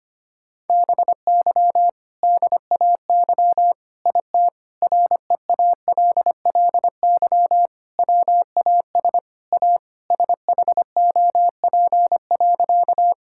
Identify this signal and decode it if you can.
Morse code, heard as a keyed tone, about 25 wpm, 700 Hz. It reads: BY DAY IT REALLY WAS A SHOP.
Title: Sentences from Top 1600 Words 25wpm